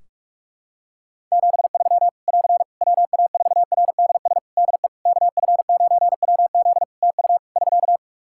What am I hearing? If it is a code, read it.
Morse code, heard as a keyed tone, about 43 wpm, 700 Hz. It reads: 73 <AR> WA4RDS DE KF9WZ TU <SK>
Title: K1USN SST QSO Practice 45wpm